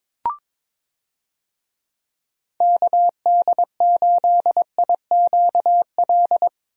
Morse code, heard as a keyed tone, about 21 wpm, 700 Hz. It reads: KD8IQL
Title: USA Call Signs - 6 Characters - Repeated 3x - 22wpm